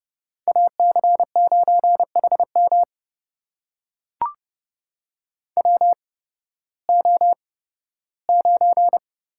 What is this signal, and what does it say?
Morse code, heard as a keyed tone, about 30 wpm, spaced as for 16 wpm, 700 Hz. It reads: AC9HM WO9
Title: USA Call Signs - 5 Characters - 8x Character Spacing - 30wpm